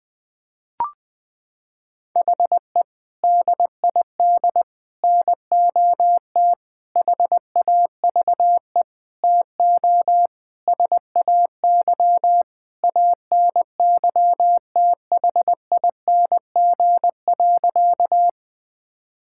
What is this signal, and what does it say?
Morse code, heard as a keyed tone, about 21 wpm, 700 Hz. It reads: HE DID NOT HAVE TO SAY ANYTHING.
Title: Sentences with 5-Word Phrases - 20wpm